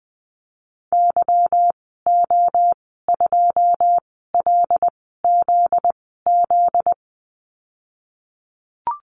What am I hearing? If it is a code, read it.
Morse code, heard as a keyed tone, about 20 wpm, spaced as for 14 wpm, 700 Hz. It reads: YO2LZZ
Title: 245 Lesson 39g - Letter z - Call Signs 20wpm